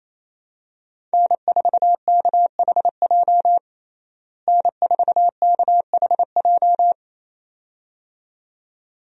Morse code, heard as a keyed tone, about 28 wpm, 700 Hz. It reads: N4KHJ N4KHJ
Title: USA Call Signs - 5 Characters - Repeated 2x - 28wpm